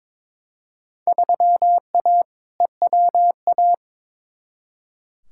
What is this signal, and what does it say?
Morse code, heard as a keyed tone, about 22 wpm, 700 Hz. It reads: 3A EWA
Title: Field Day Exchanges - 22wpm